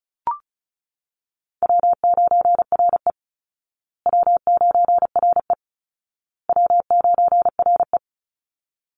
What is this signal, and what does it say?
Morse code, heard as a keyed tone, about 34 wpm, 700 Hz. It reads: W9RE W9RE W9RE
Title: Easy Call Signs Repeated 3x - 35wpm